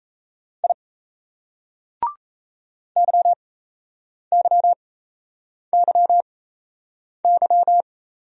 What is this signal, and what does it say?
Morse code, heard as a keyed tone, about 33 wpm, spaced as for 8 wpm, 700 Hz. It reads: I YYYY
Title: Speed-Racing Single Letters 28wpm